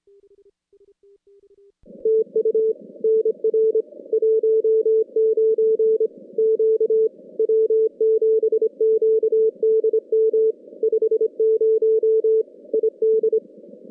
Morse code, heard as a keyed tone, about 23 wpm, 410 Hz. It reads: TU NR 19 Q W7QDM 50 ID
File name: contests / CW-SS-k5tr- -2100